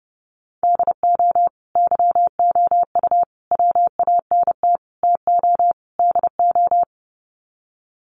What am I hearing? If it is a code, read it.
Morse code, heard as a keyed tone, about 30 wpm, 700 Hz. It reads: DO YOU WANT TO DO
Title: Sets of 5 Words Encore 30wpm